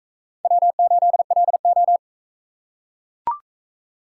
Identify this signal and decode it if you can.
Morse code, heard as a keyed tone, about 42 wpm, 700 Hz. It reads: W8LO